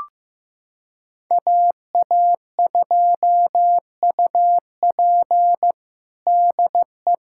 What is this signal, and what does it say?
Morse code, heard as a keyed tone, about 15 wpm, 700 Hz. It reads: AA2UP DE
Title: USA Callsign DE Callsign - 5 Characters - 15wpm